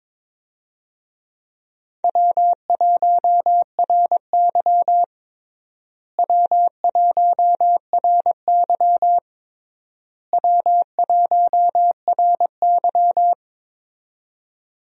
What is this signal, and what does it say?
Morse code, heard as a keyed tone, about 22 wpm, 700 Hz. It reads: W1RY W1RY W1RY
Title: USA Call Signs - 4 Characters - Repeated 3x - 22wpm